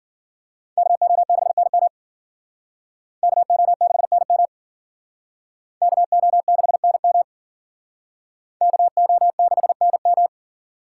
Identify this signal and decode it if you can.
Morse code, heard as a keyed tone, about 52 wpm, 700 Hz. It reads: KO6NM KO6NM KO6NM KO6NM